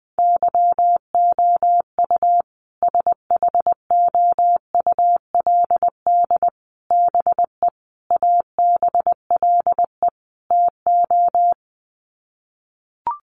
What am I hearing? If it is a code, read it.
Morse code, heard as a keyed tone, about 20 wpm, 700 Hz. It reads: YOU SHOULD BE ABLE TO
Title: Sets of 5 Words Encore 20wpm